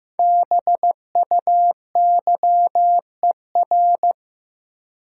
Morse code, heard as a keyed tone, about 16 wpm, 700 Hz. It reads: BUYER
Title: Five Letter Words - 15wpm